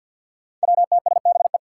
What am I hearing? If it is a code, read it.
Morse code, heard as a keyed tone, about 50 wpm, 700 Hz. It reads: WTSBE
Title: Five Challenging Letter-Number Combinations 50wpm